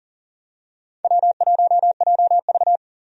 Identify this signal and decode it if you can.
Morse code, heard as a keyed tone, about 40 wpm, 700 Hz. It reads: W1JV
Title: USA Call Signs - 4 Characters - 40wpm